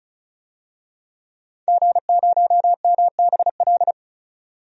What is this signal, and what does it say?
Morse code, heard as a keyed tone, about 36 wpm, 700 Hz. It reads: G0MBL
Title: DXCC-100 Call Signs - 5 Characters - 35wpm